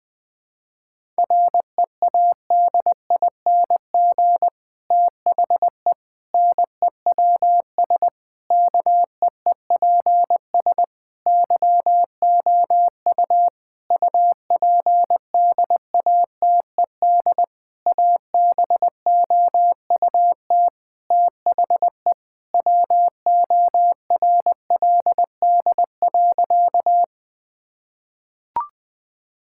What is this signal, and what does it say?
Morse code, heard as a keyed tone, about 20 wpm, 700 Hz. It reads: READING THE NEWS KEEPS YOU UPDATED ABOUT THE WORLD.
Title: Difficult Sentences with Hints - 20wpm